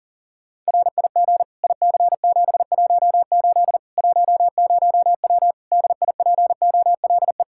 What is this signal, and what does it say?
Morse code, heard as a keyed tone, about 37 wpm, 700 Hz. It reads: RIG IC718 10W DIPOLE
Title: Practice QSO - Exchange 2 - Info - 40wpm